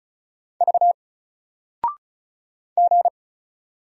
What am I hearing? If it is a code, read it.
Morse code, heard as a keyed tone, about 36 wpm, 700 Hz. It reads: V G
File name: Single-Letter-Number-35wpm